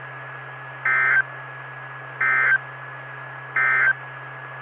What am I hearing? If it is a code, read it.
Emergency Alert System (SAME) tones, heard as harsh buzzing bursts. It is heard from 0.7 to 4.0 s.
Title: Saline, AK SKYWARN Repeater System - W5RHS